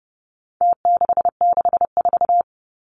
Morse code, heard as a keyed tone, about 30 wpm, 700 Hz. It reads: T664
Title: Numbers - Contest Style - 4-Digit 30wpm